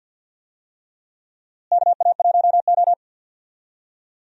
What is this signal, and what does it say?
Morse code, heard as a keyed tone, about 47 wpm, 700 Hz. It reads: KA1O
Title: USA Call Signs - 4 Characters - Repeated 3x - 50wpm